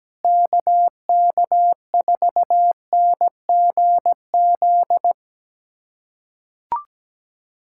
Morse code, heard as a keyed tone, about 17 wpm, 700 Hz. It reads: KK4NGZ
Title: USA Callsign DE Callsign 17wpm